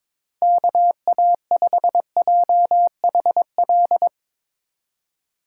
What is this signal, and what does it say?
Morse code, heard as a keyed tone, about 22 wpm, 700 Hz. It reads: KA5JHL
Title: Speed Racing - USA 2x3 Call Signs 22wpm